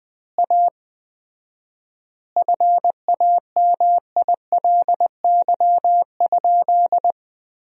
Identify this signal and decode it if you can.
Morse code, heard as a keyed tone, about 20 wpm, 700 Hz. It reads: A FAMILY?
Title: 4x Word-Spacing - Sentences from Top 400 Words - 20wpm